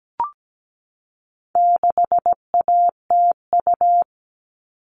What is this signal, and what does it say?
Morse code, heard as a keyed tone, about 17 wpm, 700 Hz. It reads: 6ATU
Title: All Cut Numbers - 4-Digit 17wpm